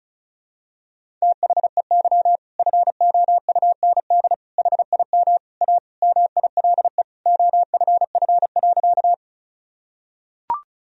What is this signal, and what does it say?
Morse code, heard as a keyed tone, about 34 wpm, 700 Hz. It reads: THEY FOUND HIM A MILE OFF.
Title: Sentences from Top 1000 Words - Review Part 1 - 35wpm